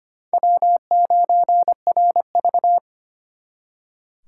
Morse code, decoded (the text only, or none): W9RV